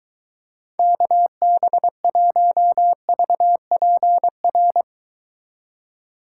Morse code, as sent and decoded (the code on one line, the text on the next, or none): -.- -... .---- ...- .--. .-.
KB1VPR